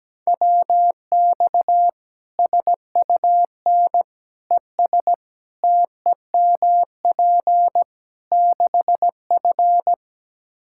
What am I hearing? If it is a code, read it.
Morse code, heard as a keyed tone, about 17 wpm, 700 Hz. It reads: WX SUN ES TEMP 6F
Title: Practice QSO - Exchange 2 - Info - 17wpm